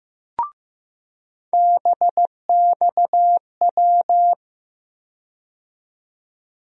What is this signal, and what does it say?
Morse code, heard as a keyed tone, about 15 wpm, 700 Hz. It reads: BXW
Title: USA Call Sign Postfixes - 3 Letters 15wpm